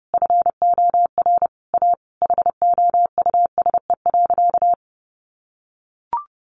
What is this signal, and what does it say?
Morse code, heard as a keyed tone, about 30 wpm, 700 Hz. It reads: FOR A HOUSE.
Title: Sentences from Top 600 Words 30wpm